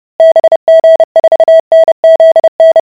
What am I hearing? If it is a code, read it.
Morse code, heard as a keyed tone, about 30 wpm, 650 Hz. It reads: BG4NZN